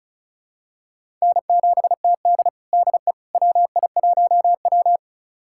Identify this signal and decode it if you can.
Morse code, heard as a keyed tone, about 36 wpm, 700 Hz. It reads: N7TD DE WI1W
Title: USA Callsign DE Callsign - 4 Characters - 35wpm